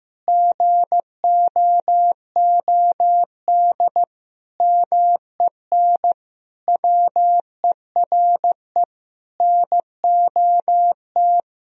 Morse code, heard as a keyed tone, about 15 wpm, 700 Hz. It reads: GOOD MEN WERE NOT
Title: Sentences from Top 1000 Words - Review Part 2 - 15wpm